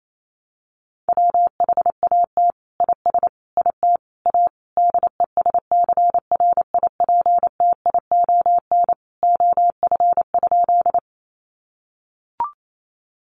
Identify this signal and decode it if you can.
Morse code, heard as a keyed tone, about 28 wpm, 700 Hz. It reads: WHAT IS IT A DESCRIPTION OF?
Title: Sentences from Top 1300 Words 28wpm